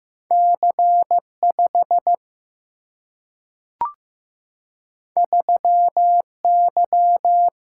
Morse code, heard as a keyed tone, about 15 wpm, 700 Hz. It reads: C 5 3 Y
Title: Single-Letter Single-Number 15wpm